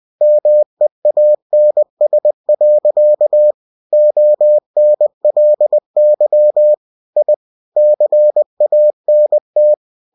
Morse code, heard as a keyed tone, about 20 wpm, 600 Hz. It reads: MEANS. ONLY I CANT